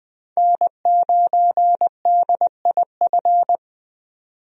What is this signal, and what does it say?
Morse code, heard as a keyed tone, about 20 wpm, 700 Hz. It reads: N9DIF
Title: USA Call Signs - 5 Characters - 20wpm